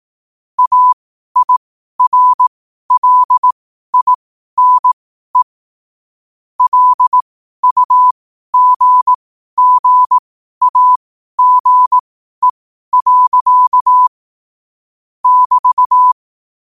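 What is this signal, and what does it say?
Morse code, heard as a keyed tone, about 18 wpm, spaced as for 12 wpm, 1000 Hz. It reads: AIRLINE LUGGAGE. <BT>
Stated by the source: Morse code sent at 12 words per minute.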